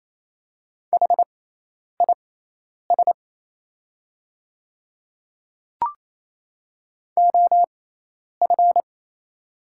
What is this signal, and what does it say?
Morse code, heard as a keyed tone, about 28 wpm, spaced as for 10 wpm, 700 Hz. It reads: HIS OF
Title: 6x Character Spacing - Top 100 Words - 28wpm